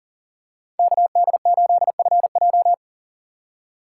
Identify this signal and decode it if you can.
Morse code, heard as a keyed tone, about 40 wpm, 700 Hz. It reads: KD8FJ
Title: USA Call Signs - 5 Characters - 2x Character Spacing - 40wpm